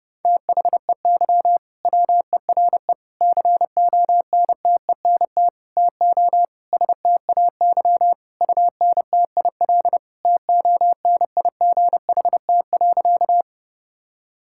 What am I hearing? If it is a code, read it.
Morse code, heard as a keyed tone, about 30 wpm, 700 Hz. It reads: THEY WERE CONTENT TO STAY UNTIL TONIGHT.
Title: Sentences from Top 1000 Words - Review Part 6 - 30wpm